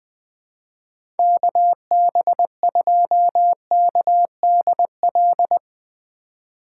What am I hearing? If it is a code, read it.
Morse code, heard as a keyed tone, about 20 wpm, 700 Hz. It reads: KB2KDL
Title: USA Call Signs - 6 Characters - 4x Character Spacing - 20wpm